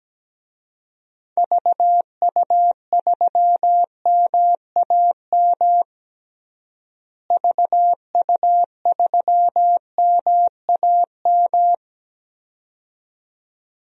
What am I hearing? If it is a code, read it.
Morse code, heard as a keyed tone, about 17 wpm, 700 Hz. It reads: VU3MAM VU3MAM
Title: DXCC-100 Call Signs - 6 Characters - Repeated 2x - 17wpm